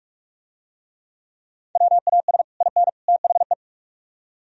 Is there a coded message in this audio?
Yes: Morse code from 1.8 to 3.5 s, heard as a keyed tone, about 42 wpm, 700 Hz. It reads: WAS IN THE